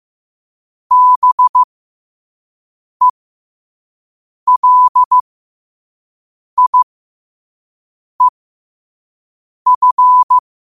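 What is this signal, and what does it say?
Morse code, heard as a keyed tone, about 15 wpm, spaced as for 5 wpm, 1000 Hz. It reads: BELIEF
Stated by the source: Morse code at 5 words per minute.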